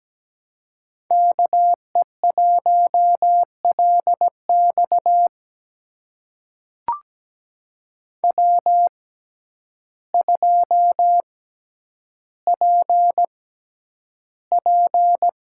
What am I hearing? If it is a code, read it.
Morse code, heard as a keyed tone, about 17 wpm, spaced as for 9 wpm, 700 Hz. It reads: KE1LX W2PP
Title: USA Call Signs - 5 Characters - 6x Character Spacing - 17wpm